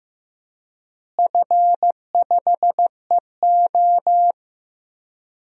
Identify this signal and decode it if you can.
Morse code, heard as a keyed tone, about 15 wpm, 700 Hz. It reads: F5EO
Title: DXCC-100 Call Signs - 4 Characters - Repeated 3x - 15wpm